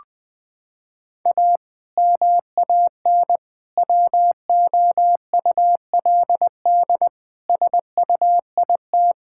Morse code, heard as a keyed tone, about 21 wpm, 700 Hz. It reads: A MAN WOULD SUIT
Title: Sentences from Top 1600 Words 20wpm